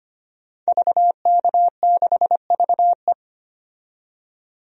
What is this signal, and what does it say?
Morse code, heard as a keyed tone, about 25 wpm, 700 Hz. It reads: VK6VE
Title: DXCC-100 Call Signs - 5 Characters - 25wpm